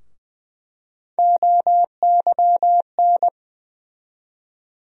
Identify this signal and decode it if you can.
Morse code, heard as a keyed tone, about 20 wpm, 700 Hz. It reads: OYN
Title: USA Call Sign Postfixes - 3 Letters 20wpm